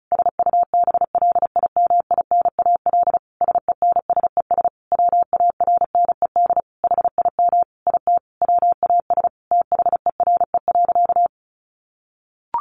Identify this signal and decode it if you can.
Morse code, heard as a keyed tone, about 34 wpm, 700 Hz. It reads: SUBLIMINAL SENSES WARNED HIM IT WAS THERE.
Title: A Sound Of Thunder 35wpm